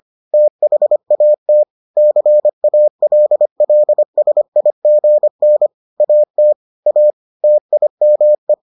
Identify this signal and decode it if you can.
Morse code, heard as a keyed tone, about 25 wpm, 600 Hz. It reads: THAT CALLSIGN AT A TIME